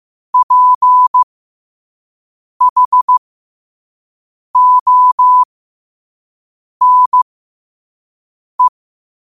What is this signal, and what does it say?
Morse code, heard as a keyed tone, about 15 wpm, spaced as for 5 wpm, 1000 Hz. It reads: PHONE
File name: qotdfiles / 5wpm